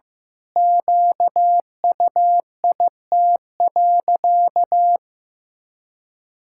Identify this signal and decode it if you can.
Morse code, heard as a keyed tone, about 15 wpm, 700 Hz. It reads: QUIT.
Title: Pangram Challenge v2 - 15wpm